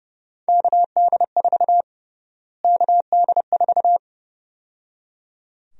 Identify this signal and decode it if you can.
Morse code, heard as a keyed tone, about 30 wpm, 700 Hz. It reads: KD4 KD4
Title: USA Call Sign Prefixes - Repeated 2x - 30wpm